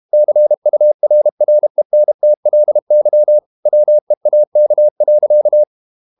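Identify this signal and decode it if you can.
Morse code, heard as a keyed tone, about 32 wpm, 600 Hz. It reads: CURRENTLY WEAK.
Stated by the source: Morse code practice at 32 wpm from CW med Gnister, reading Ken Follett's novel Never in English.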